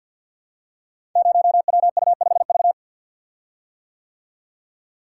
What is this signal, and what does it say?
Morse code, heard as a keyed tone, about 53 wpm, 700 Hz. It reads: 0WU5V
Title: Code Groups - 5 Letter:Numbers - 50wpm